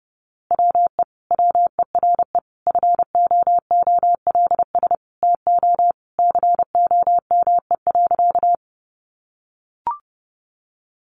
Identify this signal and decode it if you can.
Morse code, heard as a keyed tone, about 28 wpm, 700 Hz. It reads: WE WERE FOOLS TO COME.
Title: A Sound Of Thunder 30wpm